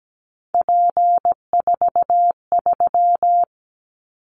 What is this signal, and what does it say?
Morse code, heard as a keyed tone, about 17 wpm, 700 Hz. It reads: P43
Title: Callsign Prefixes for Top 100 DXCC - 17wpm